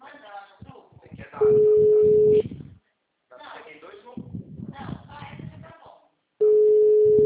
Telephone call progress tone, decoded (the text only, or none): ringback tone